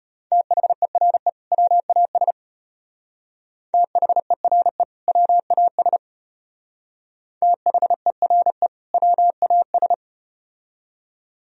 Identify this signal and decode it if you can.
Morse code, heard as a keyed tone, about 34 wpm, 700 Hz. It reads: THERE WAS THERE WAS THERE WAS